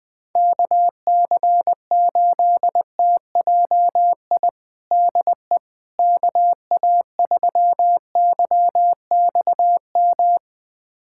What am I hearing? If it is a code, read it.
Morse code, heard as a keyed tone, about 20 wpm, 700 Hz. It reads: KC8TJI DE KA3YXM
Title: USA Callsign DE Callsign - 6 Characters - 20wpm